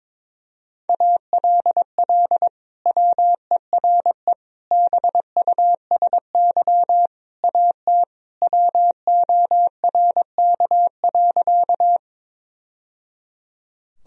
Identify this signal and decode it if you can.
Morse code, heard as a keyed tone, about 22 wpm, 700 Hz. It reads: ALL WERE BUSY AT WORK.
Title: Sentences from Top 1000 Words - Review Part 3 - 22wpm